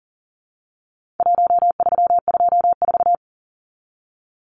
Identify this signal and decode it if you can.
Morse code, heard as a keyed tone, about 40 wpm, 700 Hz. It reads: 1324
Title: Numbers - Contest Style - 4-Digit 40wpm